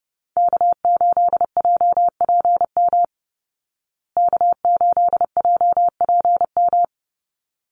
Morse code, heard as a keyed tone, about 30 wpm, 700 Hz. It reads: K8JPM K8JPM
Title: Call Signs Repeated 3x - 30wpm